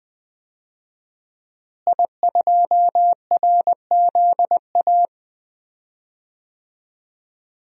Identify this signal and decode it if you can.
Morse code, heard as a keyed tone, about 20 wpm, 700 Hz. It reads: I2RZA
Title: DXCC-100 Call Signs - 5 Characters - Repeated 2x - 20wpm